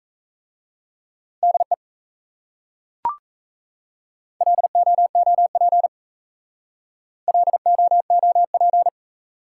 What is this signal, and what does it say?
Morse code, heard as a keyed tone, about 36 wpm, 700 Hz. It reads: DE LOOP LOOP